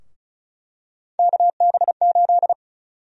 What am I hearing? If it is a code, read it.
Morse code, heard as a keyed tone, about 34 wpm, 700 Hz. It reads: KB8
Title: USA Call Sign Prefixes - Repeated 2x - 35wpm